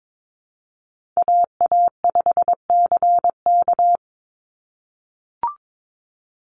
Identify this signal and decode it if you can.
Morse code, heard as a keyed tone, about 22 wpm, 700 Hz. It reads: AA5CK
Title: NA SOTA Chaser Callsigns 22wpm